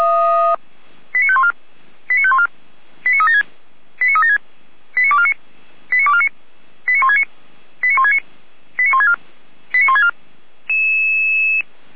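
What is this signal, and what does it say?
Five-tone selective call (ZVEI), heard as quick runs of five tones, tones of about 70 ms. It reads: 89514 89514 89376 89376 89249 89249 89169 89169 89164 89164
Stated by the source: ZVEI five-tone call 89167 received by SDR, feuerwehr-orlamuende fire brigade.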